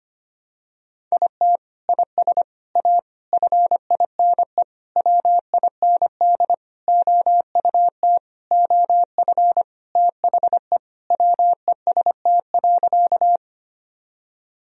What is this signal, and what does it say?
Morse code, heard as a keyed tone, about 25 wpm, 700 Hz. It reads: IT IS A FINE WIND OUT OF THE WEST.
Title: Sentences from Top 1000 Words - Review Part 2 - 25wpm